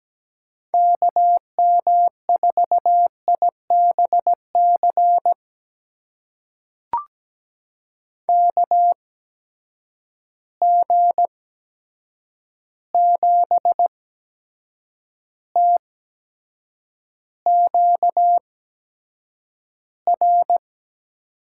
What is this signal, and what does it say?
Morse code, heard as a keyed tone, about 17 wpm, spaced as for 7 wpm, 700 Hz. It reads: KM4IBC KG7TQR
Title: USA Call Signs - 6 Characters - 8x Character Spacing - 17wpm